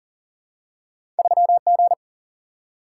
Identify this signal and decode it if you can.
Morse code, heard as a keyed tone, about 40 wpm, 700 Hz. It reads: 3G
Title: Single-Letter Single-Number 40wpm